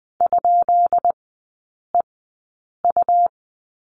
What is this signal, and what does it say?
Morse code, heard as a keyed tone, about 20 wpm, spaced as for 8 wpm, 690 Hz. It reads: ?EU
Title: 133 Lesson 22e - Question Mark - Three Characters 20wpm